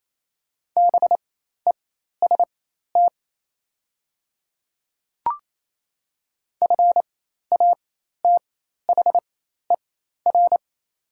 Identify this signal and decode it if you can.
Morse code, heard as a keyed tone, about 27 wpm, spaced as for 13 wpm, 700 Hz. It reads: BEST FATHER
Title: 4x Character Spacing - Top 500 Words - 28wpm